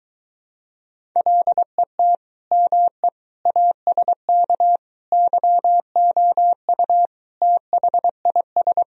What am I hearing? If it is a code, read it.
Morse code, heard as a keyed tone, about 23 wpm, 700 Hz. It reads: LET ME ASK YOU THIS